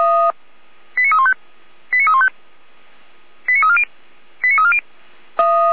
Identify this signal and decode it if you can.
Five-tone selective call (ZVEI), heard as quick runs of five tones, tones of about 70 ms. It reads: 89315 89315 89340 89340